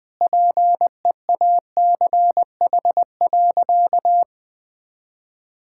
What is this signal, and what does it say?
Morse code, heard as a keyed tone, about 20 wpm, 700 Hz. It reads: PEACH.
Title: Easy Sentences with Hints - 20wpm